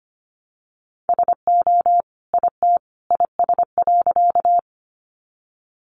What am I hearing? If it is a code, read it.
Morse code, heard as a keyed tone, about 25 wpm, 700 Hz. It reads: SO IT IS.
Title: Sentences from Top 100 Words 25wpm